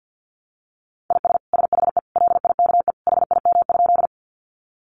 Morse code, heard as a keyed tone, about 50 wpm, 700 Hz. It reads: IS SHE LIKE HIM?
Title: Sentences from Top 100 Words 50wpm